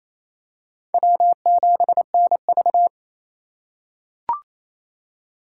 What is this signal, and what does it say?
Morse code, heard as a keyed tone, about 28 wpm, 700 Hz. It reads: W7NV
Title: USA Call Signs - 4 Characters - 2x Character Spacing - 28wpm